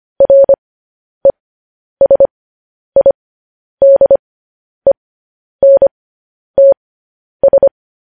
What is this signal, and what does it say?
Morse code, heard as a keyed tone, about 25 wpm, spaced as for 10 wpm, 550 Hz. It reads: RESIDENTS
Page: news headlines in morse code at 10 wpm